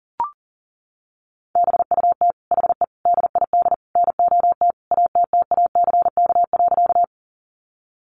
Morse code, heard as a keyed tone, about 40 wpm, 700 Hz. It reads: BUT HE DID NOT ATTACK.
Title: Sentences from Top 1500 Words 40wpm